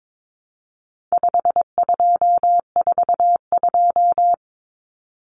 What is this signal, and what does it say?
Morse code, heard as a keyed tone, about 22 wpm, 700 Hz. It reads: 5242
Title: Numbers - 4-Digit 22wpm